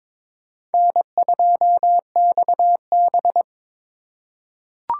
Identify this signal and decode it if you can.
Morse code, heard as a keyed tone, about 22 wpm, 700 Hz. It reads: N2XB
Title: USA Call Signs - 3x Character Spacing - 22wpm